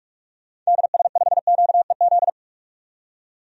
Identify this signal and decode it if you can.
Morse code, heard as a keyed tone, about 46 wpm, 700 Hz. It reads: DS5QEZ
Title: DXCC-100 Call Signs - 6 Characters - 45wpm